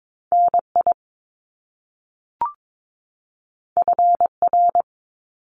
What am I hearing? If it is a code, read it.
Morse code, heard as a keyed tone, about 22 wpm, 700 Hz. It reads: NI FR
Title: Top Two Letter Combinations in Words 22wpm